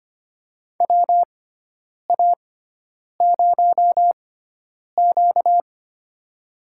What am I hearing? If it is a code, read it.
Morse code, heard as a keyed tone, about 25 wpm, spaced as for 9 wpm, 700 Hz. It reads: WA0Q
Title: USA Call Signs - 4 Characters - 6x Character Spacing - 25wpm